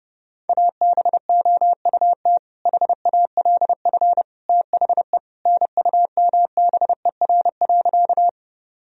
Morse code, heard as a keyed tone, about 30 wpm, 700 Hz. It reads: ABOUT HALF THE NUMBER.
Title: Sentences from Top 1000 Words - Review Part 1 - 30wpm